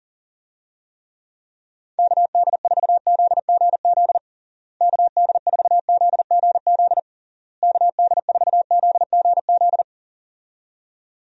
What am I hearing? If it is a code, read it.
Morse code, heard as a keyed tone, about 40 wpm, 700 Hz. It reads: KD4ZGZ KD4ZGZ KD4ZGZ
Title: USA Call Signs - 6 Characters - Repeated 3x - 40wpm